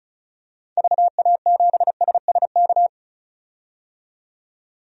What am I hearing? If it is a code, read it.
Morse code, heard as a keyed tone, about 36 wpm, 700 Hz. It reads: VA7SSK
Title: DXCC-100 Call Signs - 6 Characters - 35wpm